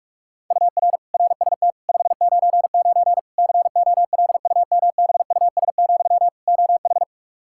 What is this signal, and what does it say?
Morse code, heard as a keyed tone, about 47 wpm, 700 Hz. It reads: UR RST 599 COLUMBUS, OH
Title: K1USN SST QSO Practice 45wpm